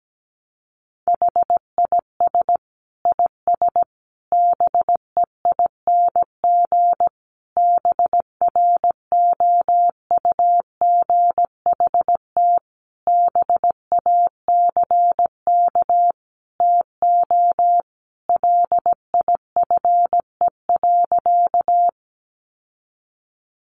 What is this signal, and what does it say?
Morse code, heard as a keyed tone, about 17 wpm, 700 Hz. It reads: HIS IS BEING BROUGHT BACK TO LIFE.
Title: Sentences from Top 600 Words 17wpm